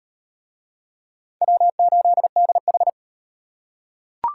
Morse code, heard as a keyed tone, about 34 wpm, 700 Hz. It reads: W8DH